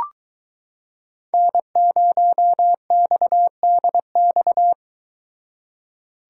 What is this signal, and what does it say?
Morse code, heard as a keyed tone, about 22 wpm, 700 Hz. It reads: N0XDX